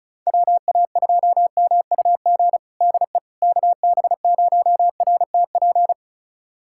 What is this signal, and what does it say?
Morse code, heard as a keyed tone, about 34 wpm, 700 Hz. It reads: WA2MUG DE KB0RTP
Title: USA Callsign DE Callsign - 6 Characters - 35wpm